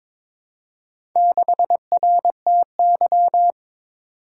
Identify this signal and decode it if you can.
Morse code, heard as a keyed tone, about 22 wpm, 700 Hz. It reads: 6RTY
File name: practice / 22wpm / Four Letter-Number Combinations - 22wpm